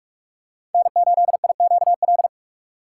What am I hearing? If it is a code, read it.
Morse code, heard as a keyed tone, about 48 wpm, 700 Hz. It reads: N8IQL
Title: USA Call Signs - 4x Character Spacing - 45wpm